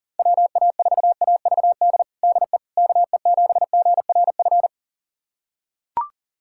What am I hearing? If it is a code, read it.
Morse code, heard as a keyed tone, about 40 wpm, 700 Hz. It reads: WA4AVD DE KE7GRF
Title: USA Callsign DE Callsign 40wpm